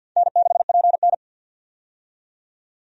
Morse code, heard as a keyed tone, about 52 wpm, 700 Hz. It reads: N6PN
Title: USA Call Signs - 4 Characters - 50wpm